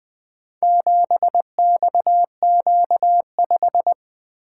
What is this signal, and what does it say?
Morse code, heard as a keyed tone, about 24 wpm, 700 Hz. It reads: 7XQ5
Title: Speed-Racing - Four Letter-Number Combinations - 20wpm